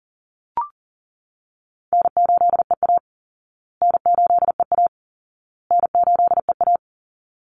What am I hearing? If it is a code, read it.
Morse code, heard as a keyed tone, about 40 wpm, 700 Hz. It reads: N8EA N8EA N8EA
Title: Call Signs Repeated 2x - 40wpm